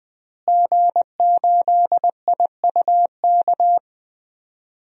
Morse code, heard as a keyed tone, about 20 wpm, 700 Hz. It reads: G8IUK
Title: DXCC-100 Call Signs - 5 Characters - Repeated 3x - 20wpm